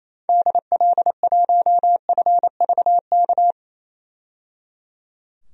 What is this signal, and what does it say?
Morse code, heard as a keyed tone, about 28 wpm, 700 Hz. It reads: DL1FVK
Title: DXCC-100 Call Signs - 6 Characters - 28wpm